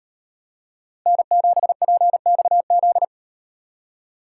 Morse code, heard as a keyed tone, about 38 wpm, 700 Hz. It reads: N7PXZ